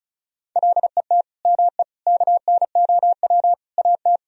Morse code, heard as a keyed tone, about 34 wpm, 700 Hz. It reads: LET ME KNOW AT
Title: Sentences from Top 300 Words and Abbreviations 35wpm